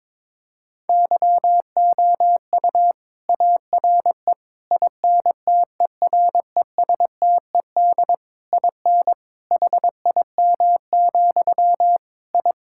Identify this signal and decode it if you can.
Morse code, heard as a keyed tone, about 22 wpm, 700 Hz. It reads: YOU ARE INTERESTED IN HIM, I
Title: Sentences from Top 1000 Words - Review Part 6 - 22wpm